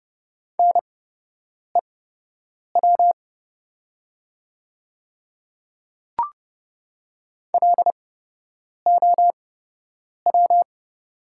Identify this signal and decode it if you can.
Morse code, heard as a keyed tone, about 30 wpm, spaced as for 8 wpm, 700 Hz. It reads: NEW LOW
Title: 8x Character Spacing - Top 300 Words - 30wpm